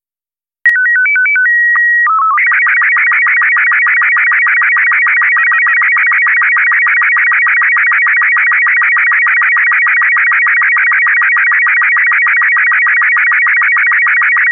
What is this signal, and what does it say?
SSTV (slow-scan television), Robot 36.